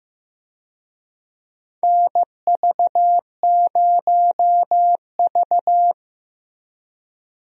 Morse code, heard as a keyed tone, about 15 wpm, 700 Hz. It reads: NV0V
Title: USA Call Signs - 4 Characters - Repeated 2x - 15wpm